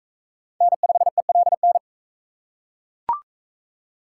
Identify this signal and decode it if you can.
Morse code, heard as a keyed tone, about 39 wpm, 700 Hz. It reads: N5ELN